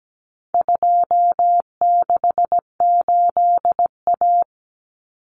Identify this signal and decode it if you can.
Morse code, heard as a keyed tone, about 17 wpm, 700 Hz. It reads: 268A
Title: Common Cut Numbers - 4-Digit 17wpm